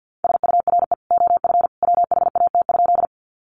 Morse code, heard as a keyed tone, about 50 wpm, 700 Hz. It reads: SURE OF WHAT?
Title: Sentences from Top 300 Words 50wpm